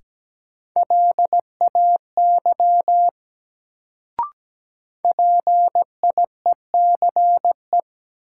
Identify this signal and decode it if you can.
Morse code, heard as a keyed tone, about 17 wpm, 700 Hz. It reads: LAY PIECE
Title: Sentences from Top 400 Words - New Word Drill - 17wpm